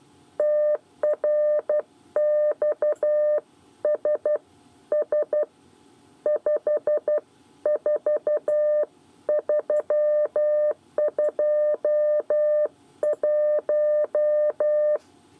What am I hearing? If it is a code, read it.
Morse code, heard as a keyed tone, about 12 wpm, 600 Hz. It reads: TRXSS 54321
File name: 8 bis lezione cw 1